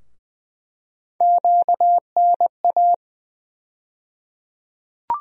Morse code, heard as a keyed tone, about 20 wpm, 700 Hz. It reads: QNA
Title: USA Call Sign Postfixes - 3 Letters 20wpm